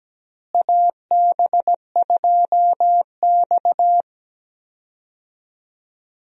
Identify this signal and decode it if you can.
Morse code, heard as a keyed tone, about 17 wpm, 700 Hz. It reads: AB2X
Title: USA Call Signs - 4 Characters - Repeated 2x - 17wpm